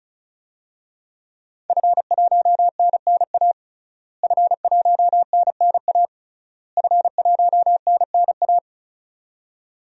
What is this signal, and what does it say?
Morse code, heard as a keyed tone, about 34 wpm, 700 Hz. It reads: F1NNA F1NNA F1NNA
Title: DXCC-100 Call Signs - 5 Characters - Repeated 3x - 35wpm